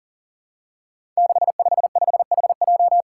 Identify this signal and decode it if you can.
Morse code, heard as a keyed tone, about 39 wpm, 700 Hz. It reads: 655HJ
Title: Five Challenging Letter-Number Combinations 40wpm